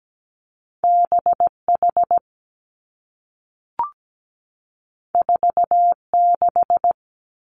Morse code, heard as a keyed tone, about 17 wpm, 700 Hz. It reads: B H 4 6
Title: Two Challenging Letter-Number Combinations 17wpm